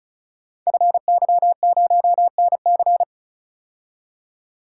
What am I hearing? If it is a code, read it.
Morse code, heard as a keyed tone, about 36 wpm, 700 Hz. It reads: FY0NC
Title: Code Groups - 5 Letter:Numbers - 35wpm